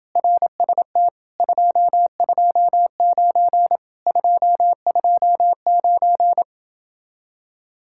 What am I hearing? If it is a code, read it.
Morse code, heard as a keyed tone, about 28 wpm, 700 Hz. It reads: RST 229 229